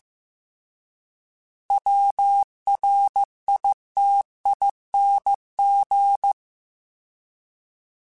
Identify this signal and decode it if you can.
Morse code, heard as a keyed tone, about 15 wpm, 800 Hz. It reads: WRITING